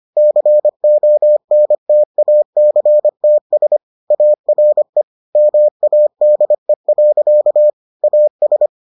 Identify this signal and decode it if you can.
Morse code, heard as a keyed tone, about 25 wpm, 600 Hz. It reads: CONTACTS ARE MADE. AS